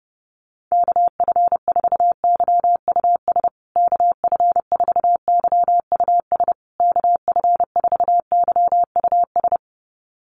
Morse code, heard as a keyed tone, about 30 wpm, 700 Hz. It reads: KF4YUS KF4YUS KF4YUS
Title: USA Call Signs Repeated 3x 30wpm